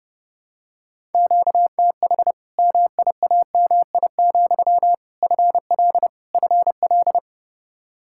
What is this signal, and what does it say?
Morse code, heard as a keyed tone, about 30 wpm, 700 Hz. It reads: QTH MIAMI, FL FL
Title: Practice QSO - Exchange 1 - Beginning - 30wpm